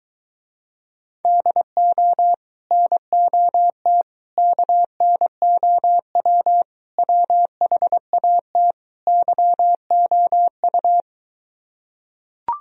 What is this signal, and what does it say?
Morse code, heard as a keyed tone, about 23 wpm, 700 Hz. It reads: DO NOT KNOW WHAT YOU